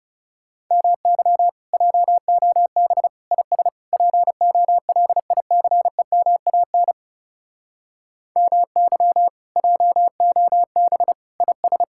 Morse code, heard as a keyed tone, about 34 wpm, 700 Hz. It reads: MY JOB IS POLICEMAN MY JOB IS
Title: Speed-Racing Common QSO Phrases - 30wpm